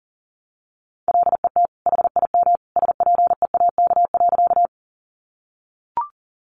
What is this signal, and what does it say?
Morse code, heard as a keyed tone, about 40 wpm, 700 Hz. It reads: LET HIM SPEAK.
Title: Sentences from Top 1100 Words 40wpm